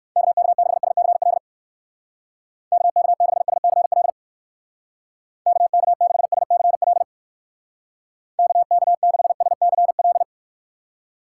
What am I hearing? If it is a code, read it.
Morse code, heard as a keyed tone, about 58 wpm, 700 Hz. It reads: KK6SCL KK6SCL KK6SCL KK6SCL